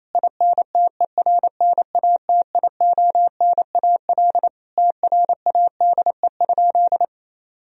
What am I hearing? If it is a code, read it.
Morse code, heard as a keyed tone, about 28 wpm, 700 Hz. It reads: INTERNATIONAL TRADE?
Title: Sentences from Top 1000 Words - Review Part 4 - 28wpm